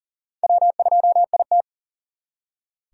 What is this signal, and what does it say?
Morse code, heard as a keyed tone, about 53 wpm, spaced as for 46 wpm, 700 Hz. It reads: W2IT
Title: Speed Racing - USA 1x2 Call Signs 40wpm